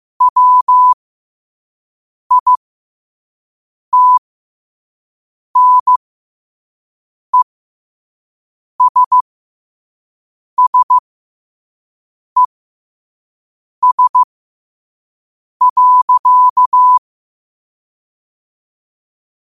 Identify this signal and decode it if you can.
Morse code, heard as a keyed tone, about 15 wpm, spaced as for 5 wpm, 1000 Hz. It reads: WITNESSES.
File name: qotdfiles / 5wpm